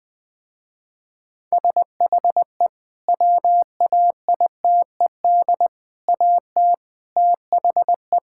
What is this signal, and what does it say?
Morse code, heard as a keyed tone, about 20 wpm, 700 Hz. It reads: SHE WAITED AT THE
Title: Sentences with 5-Word Phrases - Spoken Beginning - 20wpm